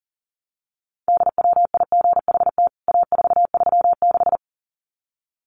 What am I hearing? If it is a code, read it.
Morse code, heard as a keyed tone, about 37 wpm, 700 Hz. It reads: DWIGHT A436
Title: CWT Exchange 40wpm